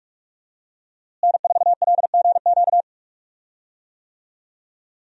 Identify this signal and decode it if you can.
Morse code, heard as a keyed tone, about 46 wpm, 700 Hz. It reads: N4LGQ
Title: USA Call Signs - 5 Characters - Repeated 2x - 45wpm